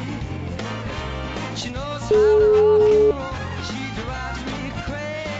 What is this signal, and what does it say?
Ringback tone (Europe, 425 Hz), with noise.